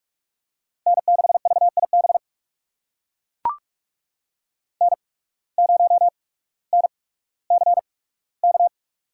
Morse code, heard as a keyed tone, about 41 wpm, spaced as for 19 wpm, 700 Hz. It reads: N6VIB N0NCK
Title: USA Call Signs - 5 Characters - 8x Character Spacing - 45wpm